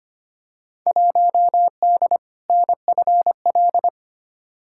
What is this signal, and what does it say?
Morse code, heard as a keyed tone, about 25 wpm, 700 Hz. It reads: 1D NFL
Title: Field Day Exchanges - 25wpm